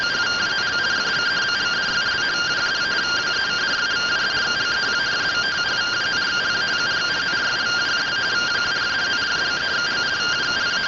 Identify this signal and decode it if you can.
Two-tone teleprinter signal, about 170 Hz shift.